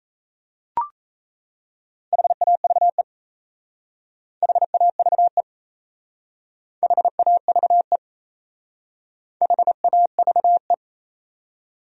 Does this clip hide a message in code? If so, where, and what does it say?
Yes: Morse code from 2.1 to 10.7 s, heard as a keyed tone, about 36 wpm, 700 Hz. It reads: HAVE HAVE HAVE HAVE